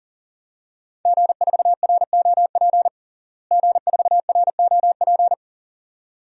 Morse code, heard as a keyed tone, about 40 wpm, 700 Hz. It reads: G4ROP G4ROP
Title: DXCC-100 Call Signs - 5 Characters - Repeated 2x - 40wpm